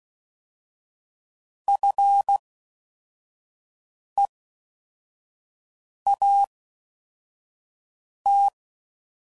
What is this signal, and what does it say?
Morse code, heard as a keyed tone, about 16 wpm, spaced as for 4 wpm, 800 Hz. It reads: FEAT